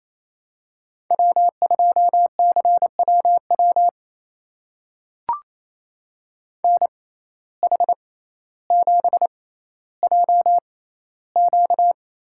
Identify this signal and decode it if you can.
Morse code, heard as a keyed tone, about 28 wpm, spaced as for 14 wpm, 700 Hz. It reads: W2CWW NH7JQ
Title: USA Call Signs - 5 Characters - 6x Character Spacing - 28wpm